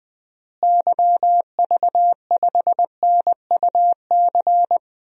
Morse code, heard as a keyed tone, about 20 wpm, 700 Hz. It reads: YV5NUC
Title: DXCC-100 Call Signs - 6 Characters - Repeated 2x - 20wpm